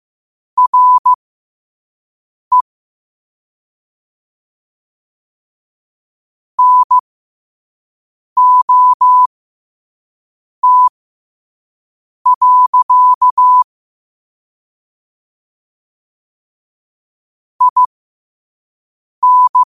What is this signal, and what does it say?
Morse code, heard as a keyed tone, about 15 wpm, spaced as for 5 wpm, 1000 Hz. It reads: RE NOT. IN
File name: qotdfiles / 5wpm